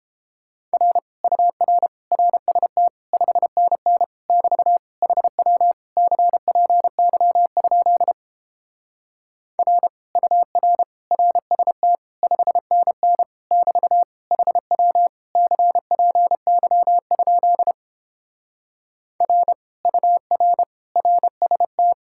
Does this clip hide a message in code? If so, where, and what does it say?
Yes: Morse code from 0.7 to 21.9 s, heard as a keyed tone, about 29 wpm, 700 Hz. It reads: R UR RST 5NN <BT> HW CPY? R UR RST 5NN <BT> HW CPY? R UR RST